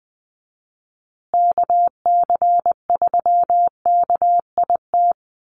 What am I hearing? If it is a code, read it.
Morse code, heard as a keyed tone, about 20 wpm, 700 Hz. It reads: KC3KIT
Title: USA Call Signs Repeated 3x 20wpm